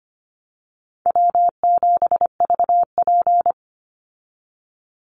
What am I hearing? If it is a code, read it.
Morse code, heard as a keyed tone, about 25 wpm, 700 Hz. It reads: W7VP
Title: Call Signs Repeated 2x - 25wpm